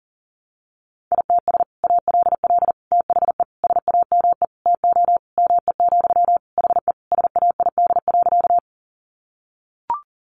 Morse code, heard as a keyed tone, about 40 wpm, 700 Hz. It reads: ITS ALL THE SAME TO ME, HE SAID.
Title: Sentences from Top 600 Words 40wpm